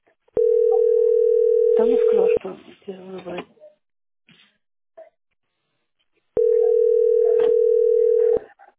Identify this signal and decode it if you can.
Ringback tone (North America), 440 + 480 Hz, with speech.
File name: f6cf4b37-d546-45c3-9eb4